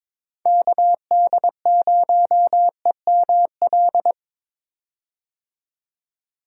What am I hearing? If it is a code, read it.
Morse code, heard as a keyed tone, about 22 wpm, 700 Hz. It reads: KD0EML
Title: USA Call Signs - 6 Characters - 22wpm